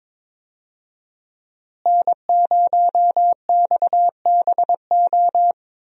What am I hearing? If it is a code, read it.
Morse code, heard as a keyed tone, about 22 wpm, 700 Hz. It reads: N0XBO
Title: USA Call Signs - 5 Characters - Repeated 2x - 22wpm